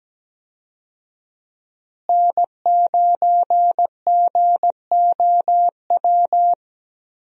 Morse code, heard as a keyed tone, about 17 wpm, 700 Hz. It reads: N9GOW
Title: USA Call Signs - 5 Characters - Repeated 2x - 17wpm